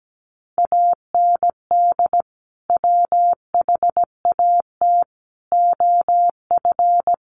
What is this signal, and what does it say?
Morse code, heard as a keyed tone, about 17 wpm, 700 Hz. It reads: AND WHAT OF
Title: Sentences from Top 900 Words 17wpm